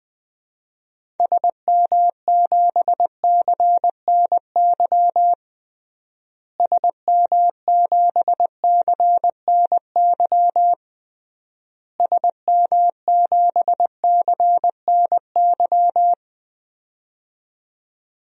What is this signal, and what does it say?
Morse code, heard as a keyed tone, about 21 wpm, 700 Hz. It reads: SM7CNY SM7CNY SM7CNY
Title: DXCC-100 Call Signs - 6 Characters - Repeated 3x - 20wpm